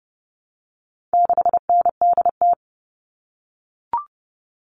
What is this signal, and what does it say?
Morse code, heard as a keyed tone, about 30 wpm, 700 Hz. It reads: 6NDT
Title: All Cut Numbers - 4-Digit 30wpm